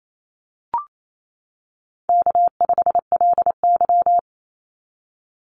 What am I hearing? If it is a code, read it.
Morse code, heard as a keyed tone, about 28 wpm, 700 Hz. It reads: K5LY
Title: Easy Call Signs - 28wpm